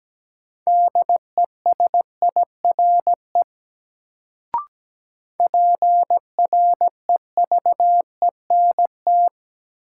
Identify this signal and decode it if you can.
Morse code, heard as a keyed tone, about 17 wpm, 700 Hz. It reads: DESIRE PREVENT
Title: Sentences from Top 1100 Words - New Word Drill - 17wpm